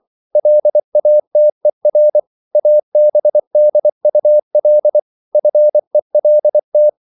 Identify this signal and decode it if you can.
Morse code, heard as a keyed tone, about 24 wpm, 600 Hz. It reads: LATER ABDUL FELT